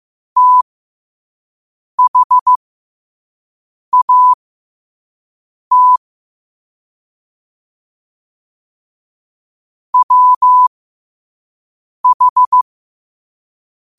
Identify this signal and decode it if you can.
Morse code, heard as a keyed tone, about 15 wpm, spaced as for 5 wpm, 1000 Hz. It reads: THAT WH
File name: qotdfiles / 5wpm